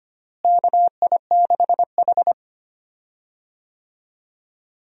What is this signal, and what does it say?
Morse code, heard as a keyed tone, about 25 wpm, 700 Hz. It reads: KI6H
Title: USA Call Signs - 4 Characters - Repeated 3x - 25wpm